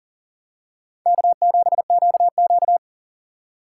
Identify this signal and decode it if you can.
Morse code, heard as a keyed tone, about 40 wpm, 700 Hz. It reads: K7QQ
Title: USA Call Signs - 4 Characters - 4x Character Spacing - 40wpm